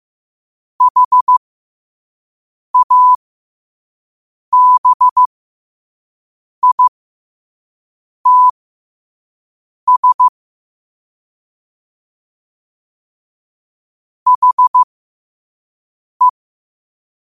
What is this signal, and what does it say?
Morse code, heard as a keyed tone, about 15 wpm, spaced as for 5 wpm, 1000 Hz. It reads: HABITS HE
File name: qotdfiles / 5wpm